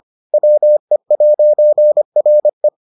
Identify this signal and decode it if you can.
Morse code, heard as a keyed tone, about 25 wpm, 600 Hz. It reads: WE'RE